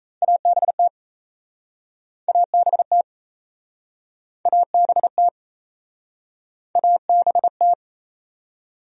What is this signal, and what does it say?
Morse code, heard as a keyed tone, about 37 wpm, 700 Hz. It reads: ABT ABT ABT ABT